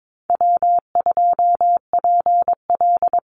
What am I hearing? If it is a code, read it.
Morse code, heard as a keyed tone, about 22 wpm, 700 Hz. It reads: W2PL
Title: Call Signs - 22wpm